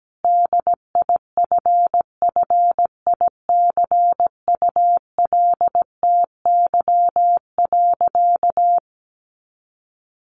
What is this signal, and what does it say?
Morse code, heard as a keyed tone, about 17 wpm, 700 Hz. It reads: DIFFICULTY.
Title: Sentences from Top 1500 Words 17wpm